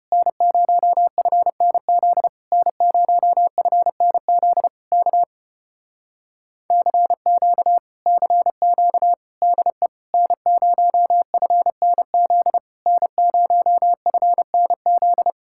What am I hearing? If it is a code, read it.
Morse code, heard as a keyed tone, about 33 wpm, 700 Hz. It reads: N0FNZ N0FNZ K CQ CQ DE N0FNZ N0FNZ